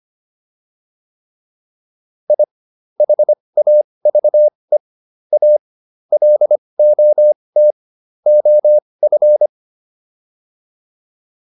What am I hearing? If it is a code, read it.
Morse code, heard as a keyed tone, about 25 wpm, spaced as for 20 wpm, 600 Hz. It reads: I HAVE A LOT OF